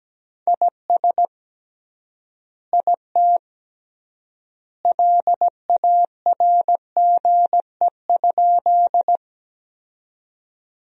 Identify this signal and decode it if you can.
Morse code, heard as a keyed tone, about 17 wpm, 700 Hz. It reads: IS IT LARGE?
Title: 3x Word-Spacing - Sentences from Top 200 Words - 17wpm